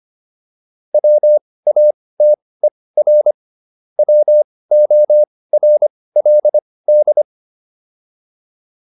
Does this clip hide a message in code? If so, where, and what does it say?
Yes: Morse code from 0.9 to 7.2 s, heard as a keyed tone, about 25 wpm, spaced as for 18 wpm, 600 Hz. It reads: WATER WORLD